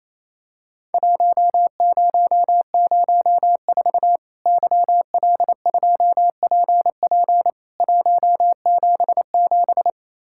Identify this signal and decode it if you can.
Morse code, heard as a keyed tone, about 28 wpm, 700 Hz. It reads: 1004 YL2PP 177
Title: Worked All Europe DX Contest - QTC Exchange - 28wpm